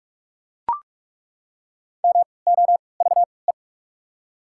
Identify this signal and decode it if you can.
Morse code, heard as a keyed tone, about 43 wpm, 690 Hz. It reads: M O V E
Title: 3x Character Spacing - Top 200 Words - 45wpm